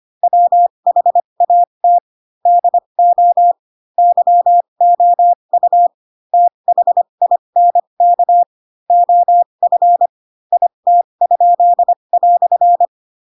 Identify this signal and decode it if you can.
Morse code, heard as a keyed tone, about 25 wpm, spaced as for 22 wpm, 700 Hz. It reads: WHAT DO YOU THINK OF IT?"
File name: Pinocchio / Pinocchio ch2 18 25 ik1hge